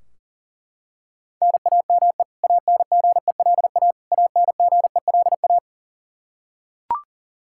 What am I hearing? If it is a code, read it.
Morse code, heard as a keyed tone, about 37 wpm, 700 Hz. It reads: NAME ANGELA ANGELA
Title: Practice QSO - Exchange 1 - Beginning - 40wpm